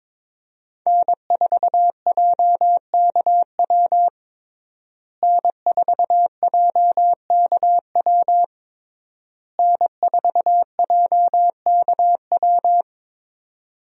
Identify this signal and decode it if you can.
Morse code, heard as a keyed tone, about 22 wpm, 700 Hz. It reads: N4JKW N4JKW N4JKW
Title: USA Call Signs - 5 Characters - Repeated 3x - 22wpm